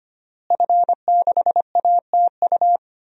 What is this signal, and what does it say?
Morse code, heard as a keyed tone, about 25 wpm, 700 Hz. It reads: F6ATU
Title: DXCC-100 Call Signs - 5 Characters - Repeated 3x - 25wpm